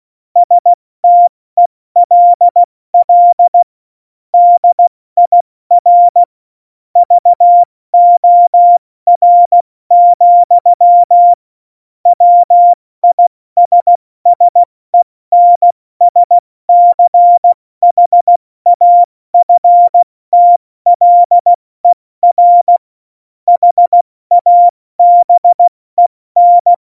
Morse code, heard as a keyed tone, about 16 wpm, 700 Hz. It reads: STELL DIR VOR, WISSENSCHAFTLER HABEN
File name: Morsekurs / Witze / Witz 08. Januar 2019, 70 F80